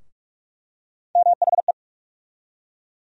Morse code, heard as a keyed tone, about 42 wpm, 700 Hz. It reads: MHE
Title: USA Call Sign Postfixes - 3 Letters 45wpm